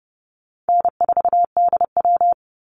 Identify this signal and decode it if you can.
Morse code, heard as a keyed tone, about 30 wpm, 700 Hz. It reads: N4DW
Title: Call Signs - 30wpm